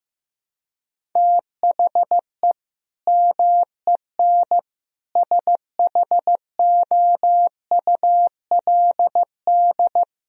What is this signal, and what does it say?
Morse code, heard as a keyed tone, about 15 wpm, 700 Hz. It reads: THE MEN SHOULD
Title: Sentences from Top 1000 Words - Review Part 1 - 15wpm